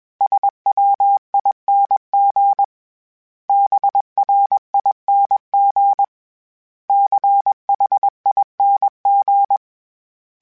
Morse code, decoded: SWING BRING CHING

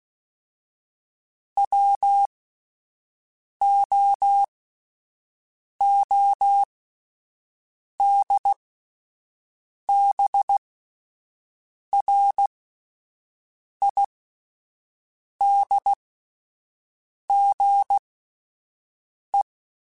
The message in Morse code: WOODBRIDGE

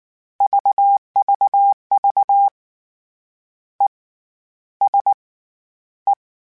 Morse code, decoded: VVV ESE

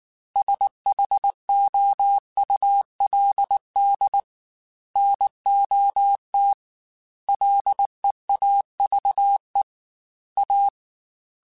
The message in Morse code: SHOULD NOT LEAVE A

